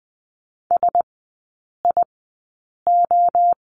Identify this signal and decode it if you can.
Morse code, heard as a keyed tone, about 20 wpm, 700 Hz. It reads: S I O
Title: 035 Lesson 7e - Letter s - Three Characters 20wpm